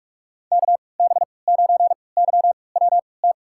Morse code, heard as a keyed tone, about 42 wpm, spaced as for 25 wpm, 700 Hz. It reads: KB9YWT
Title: USA Call Signs - 6 Characters - 3x Character Spacing - 45wpm